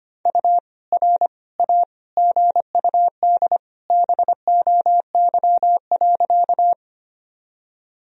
Morse code, heard as a keyed tone, about 25 wpm, 700 Hz. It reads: U R A GUD BOY.
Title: Sentences from Top 200 Words and Abbreviations 25wpm